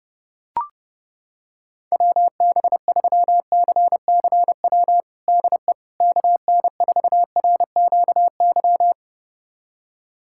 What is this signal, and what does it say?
Morse code, heard as a keyed tone, about 32 wpm, 700 Hz. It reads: WB3CCW DE KN4RQY
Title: USA Callsign DE Callsign - 6 Characters - 30wpm